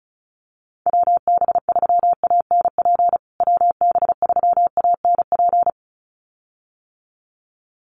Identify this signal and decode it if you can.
Morse code, heard as a keyed tone, about 34 wpm, 700 Hz. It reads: WB3ANP WB3ANP
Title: USA Call Signs Repeated 2x 35wpm